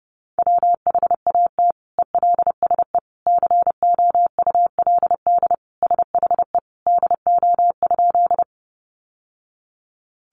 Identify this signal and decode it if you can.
Morse code, heard as a keyed tone, about 30 wpm, 700 Hz. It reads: WHAT ELSE COULD SHE DO?
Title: Sentences from Top 800 Words 30wpm